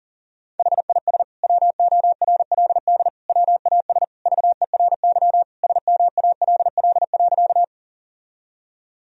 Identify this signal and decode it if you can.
Morse code, heard as a keyed tone, about 40 wpm, 700 Hz. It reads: HIS WORLD WAS VERY SMALL.
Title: Sentences from Top 1000 Words - Review Part 1 - 40wpm